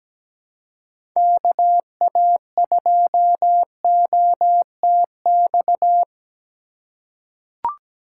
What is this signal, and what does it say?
Morse code, heard as a keyed tone, about 17 wpm, 700 Hz. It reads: KA2OTX
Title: USA Call Signs - 6 Characters - 4x Character Spacing - 17wpm